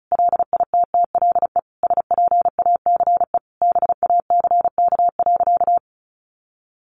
Morse code, heard as a keyed tone, about 34 wpm, 700 Hz. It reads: LITTLE SPACE BACK.
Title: Sentences from Top 1500 Words 35wpm